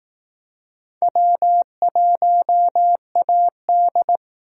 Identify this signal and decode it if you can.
Morse code, heard as a keyed tone, about 18 wpm, 700 Hz. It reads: W1AD
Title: Speed Racing - USA 1x2 Call Signs 15wpm